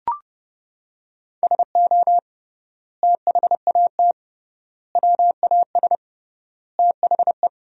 Morse code, heard as a keyed tone, about 30 wpm, 700 Hz. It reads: SO THAT WAS THE
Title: 3x Word-Spacing - Sentences from Top 100 Words - 30wpm